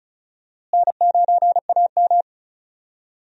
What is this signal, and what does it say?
Morse code, heard as a keyed tone, about 34 wpm, 700 Hz. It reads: N9AM
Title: USA Call Signs - 4 Characters - Repeated 2x - 35wpm